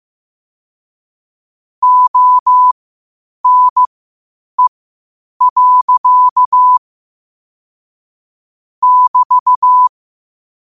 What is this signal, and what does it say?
Morse code, heard as a keyed tone, about 15 wpm, spaced as for 8 wpm, 1000 Hz. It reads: ONE. <BT>
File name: qotdfiles / 8wpm